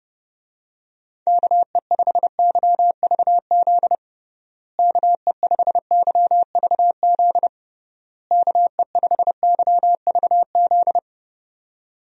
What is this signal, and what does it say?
Morse code, heard as a keyed tone, about 31 wpm, 700 Hz. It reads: KE5YVZ KE5YVZ KE5YVZ
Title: USA Call Signs - 6 Characters - Repeated 3x - 30wpm